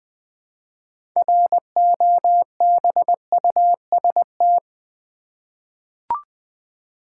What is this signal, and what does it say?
Morse code, heard as a keyed tone, about 20 wpm, 700 Hz. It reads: ROBUST
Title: 6 Letter Words - First 3 Letters Given - 20wpm